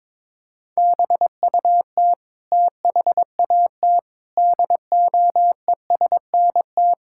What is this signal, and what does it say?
Morse code, heard as a keyed tone, about 22 wpm, 700 Hz. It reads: BUT THAT DOESNT
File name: practice / 22wpm / Sets of 7 Words - 22wpm